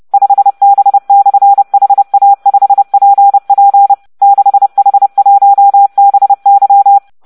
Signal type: Morse code